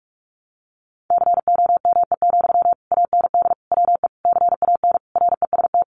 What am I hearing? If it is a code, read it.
Morse code, heard as a keyed tone, about 44 wpm, 700 Hz. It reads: COME, AND WE CAN REST
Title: Sentences from Top 500 Words 45wpm